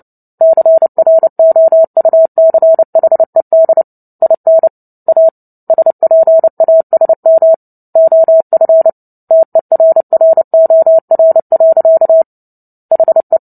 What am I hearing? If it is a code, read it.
Morse code, heard as a keyed tone, about 30 wpm, 650 Hz. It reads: CROUCHED IN A SPASM OF TERROR. HE
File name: Book Two / 03-Chapter 3 - 29 WPM